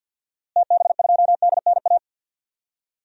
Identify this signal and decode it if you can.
Morse code, heard as a keyed tone, about 46 wpm, 700 Hz. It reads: TB2DNA
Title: DXCC-100 Call Signs - 6 Characters - Repeated 2x - 50wpm